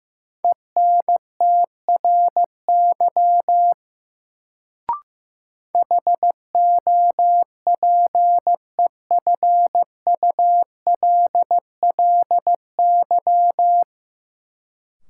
Morse code, heard as a keyed tone, about 15 wpm, 700 Hz. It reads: ENTRY HOPEFULLY
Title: Sentences from Top 1500 Words - New Word Drill Alt - 15wpm